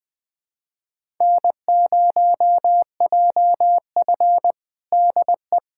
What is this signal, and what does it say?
Morse code, heard as a keyed tone, about 20 wpm, 700 Hz. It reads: N0JF DE
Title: USA Callsign DE Callsign - 4 Characters - 20wpm